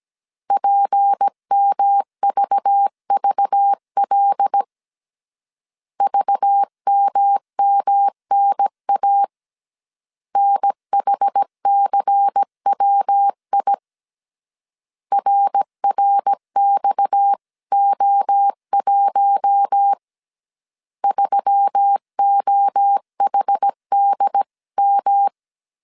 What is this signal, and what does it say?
Morse code, heard as a keyed tone, about 17 wpm, 800 Hz. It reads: PMVVL VMMNA NHCWI RRXO1 3OHDM
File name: CW / DET / 1 6 1500C S A 16wpm 2635